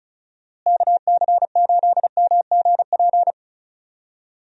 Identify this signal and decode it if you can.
Morse code, heard as a keyed tone, about 34 wpm, 700 Hz. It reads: KC8MGP
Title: USA Call Signs - 6 Characters - Repeated 2x - 35wpm